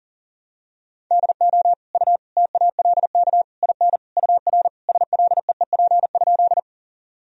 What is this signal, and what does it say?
Morse code, heard as a keyed tone, about 40 wpm, 700 Hz. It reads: DO U TALK IN UR SLEEP?
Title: Sentences from Top 500 Words and Abbreviations 40wpm